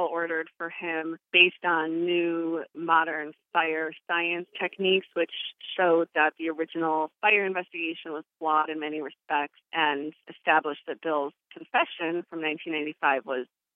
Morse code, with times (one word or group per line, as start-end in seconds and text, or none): none